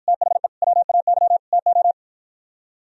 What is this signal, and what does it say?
Morse code, heard as a keyed tone, about 51 wpm, 700 Hz. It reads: THE WAY TO